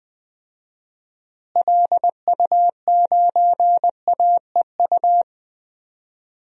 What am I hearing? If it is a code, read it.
Morse code, heard as a keyed tone, about 20 wpm, 700 Hz. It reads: LU9AEU
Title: DXCC-100 Call Signs - 6 Characters - Repeated 2x - 20wpm